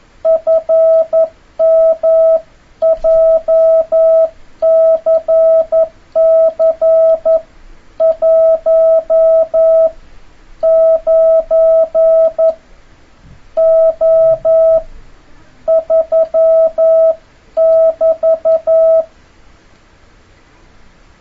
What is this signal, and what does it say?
Morse code, heard as a keyed tone, about 11 wpm, spaced as for 8 wpm, 650 Hz. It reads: FM JCC 19O3 <BT>
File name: jh8cbh / 2026cw